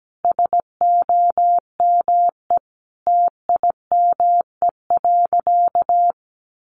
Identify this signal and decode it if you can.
Morse code, heard as a keyed tone, about 17 wpm, 700 Hz. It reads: SOME TIME.
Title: Sentences from Top 300 Words 17wpm